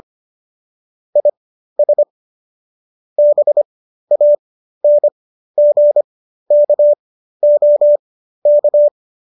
Morse code, heard as a keyed tone, about 25 wpm, spaced as for 13 wpm, 590 Hz. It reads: IS BANGKOK